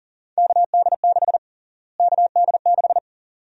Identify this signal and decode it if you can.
Morse code, heard as a keyed tone, about 40 wpm, 700 Hz. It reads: KD6 KD6
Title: USA Call Sign Prefixes - Repeated 2x - 40wpm